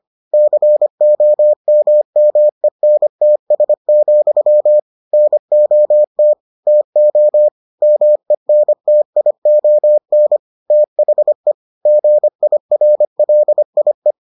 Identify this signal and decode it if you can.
Morse code, heard as a keyed tone, about 25 wpm, 600 Hz. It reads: COMMENTS, NOT TO MENTION THE GIRLIE